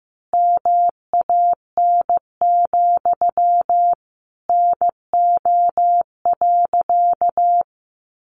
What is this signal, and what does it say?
Morse code, heard as a keyed tone, about 14 wpm, 700 Hz. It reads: MAN, NO.
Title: A Sound Of Thunder 15wpm